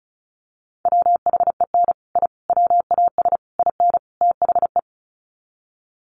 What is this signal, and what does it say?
Morse code, heard as a keyed tone, about 34 wpm, 700 Hz. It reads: WHEN I WAS IN THE
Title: Sets of 5 Words Encore2 35wpm